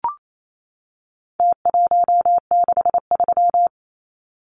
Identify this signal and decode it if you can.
Morse code, heard as a keyed tone, about 28 wpm, 700 Hz. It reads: T163
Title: Numbers - Contest Style - 4-Digit 28wpm